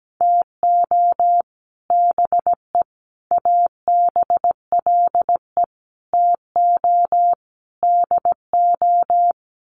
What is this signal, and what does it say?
Morse code, heard as a keyed tone, about 17 wpm, 700 Hz. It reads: TO BE ABLE TO DO
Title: Sets of 5 Words 17wpm